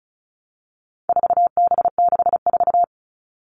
Morse code, heard as a keyed tone, about 34 wpm, 700 Hz. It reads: 4B64
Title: All Cut Numbers - 4-Digit 35wpm